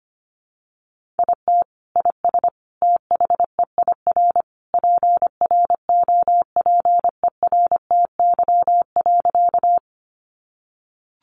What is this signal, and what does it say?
Morse code, heard as a keyed tone, about 25 wpm, 700 Hz. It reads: IT IS THEIR PROPERTY.
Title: Sentences from Top 1000 Words 25wpm